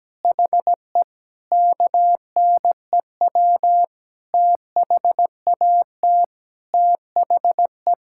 Morse code, heard as a keyed tone, about 17 wpm, 700 Hz. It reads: HE KNEW THAT THE
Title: Sentences from Top 1000 Words - Review Part 6 - 17wpm